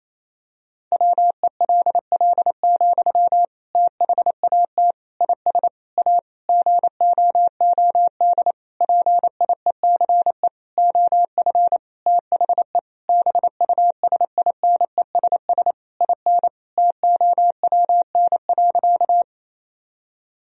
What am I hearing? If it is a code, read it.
Morse code, heard as a keyed tone, about 27 wpm, 700 Hz. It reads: WELL, THAT IS A GOOD PIECE OF THE BUSINESS IN TOWN.
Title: Sentences from Top 1000 Words - Review Part 2 - 28wpm